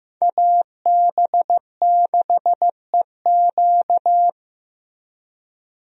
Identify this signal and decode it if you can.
Morse code, heard as a keyed tone, about 15 wpm, 700 Hz. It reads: AB6EQ
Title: USA Call Signs - 5 Characters - Repeated 3x - 15wpm